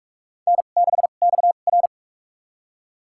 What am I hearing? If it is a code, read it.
Morse code, heard as a keyed tone, about 44 wpm, spaced as for 32 wpm, 700 Hz. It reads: N6XR
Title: USA Call Signs - 4 Characters - 2x Character Spacing - 45wpm